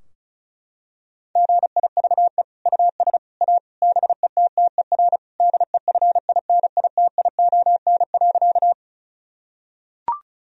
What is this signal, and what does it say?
Morse code, heard as a keyed tone, about 34 wpm, 700 Hz. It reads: GIVE US A BETTER DEFINITION.
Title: Sentences from Top 1000 Words - Review Part 4 - 35wpm